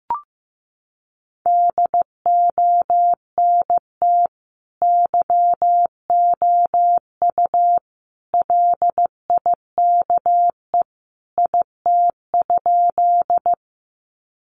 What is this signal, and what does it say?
Morse code, heard as a keyed tone, about 15 wpm, 700 Hz. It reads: DONT YOU LIKE IT?
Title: Sentences from Top 300 Words 15wpm